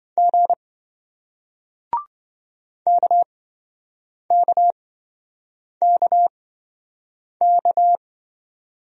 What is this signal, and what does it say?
Morse code, heard as a keyed tone, about 28 wpm, spaced as for 7 wpm, 700 Hz. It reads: G KKKK